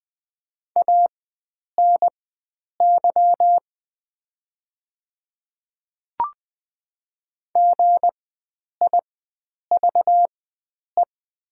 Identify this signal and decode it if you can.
Morse code, heard as a keyed tone, about 20 wpm, spaced as for 9 wpm, 700 Hz. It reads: ANY GIVE
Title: 4x Character Spacing - Top 500 Words - 20wpm